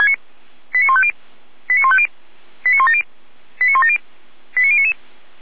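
Five-tone selective call (ZVEI), heard as quick runs of five tones, tones of about 70 ms. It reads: 89150 89150 89170 89170 89099
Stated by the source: ZVEI five-tone selective call received on an SDR, recorded by feuerwehr-orlamuende.org fire brigade.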